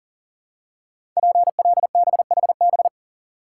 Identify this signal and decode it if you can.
Morse code, heard as a keyed tone, about 42 wpm, 700 Hz. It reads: PLBHB
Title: Code Groups - 5 Letter:Numbers - 40wpm